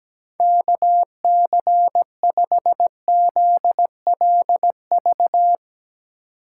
Morse code, heard as a keyed tone, about 17 wpm, 700 Hz. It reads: KC5ZLV
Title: USA Call Signs - 6 Characters - 3x Character Spacing - 17wpm